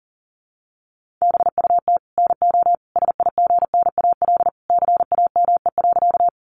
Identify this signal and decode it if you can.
Morse code, heard as a keyed tone, about 40 wpm, 700 Hz. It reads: BUT NO SIGNAL CAME.
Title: Sentences from Top 1400 Words 40wpm